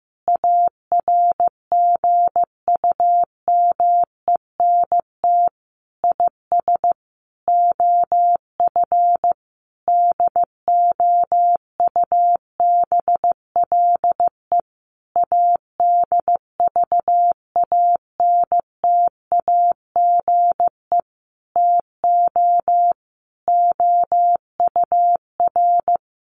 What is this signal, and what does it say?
Morse code, heard as a keyed tone, about 15 wpm, 700 Hz. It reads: ARGUMENT IS OF DOUBLE ADVANTAGE TO OUR
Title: Sentences from Top 1400 Words 15wpm